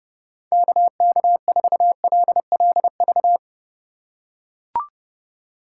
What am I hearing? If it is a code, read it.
Morse code, heard as a keyed tone, about 32 wpm, 700 Hz. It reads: KK4LLV
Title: USA Call Signs - 4x Character Spacing - 30wpm